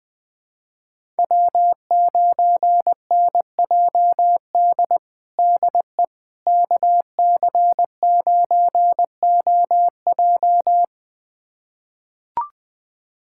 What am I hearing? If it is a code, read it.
Morse code, heard as a keyed tone, about 20 wpm, 700 Hz. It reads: W9NJD DE KC9OJ
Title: USA Callsign DE Callsign - 5 Characters - 20wpm